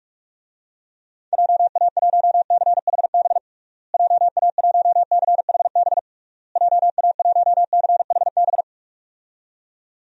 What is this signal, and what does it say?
Morse code, heard as a keyed tone, about 46 wpm, 700 Hz. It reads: JA1CHB JA1CHB JA1CHB
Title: DXCC-100 Call Signs - 6 Characters - Repeated 3x - 45wpm